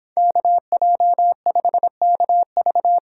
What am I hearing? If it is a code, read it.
Morse code, heard as a keyed tone, about 24 wpm, 700 Hz. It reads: KJ5KV